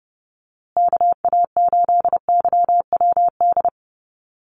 Morse code, heard as a keyed tone, about 30 wpm, 700 Hz. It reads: KA8YWD
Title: USA Call Signs Repeated 2x 30wpm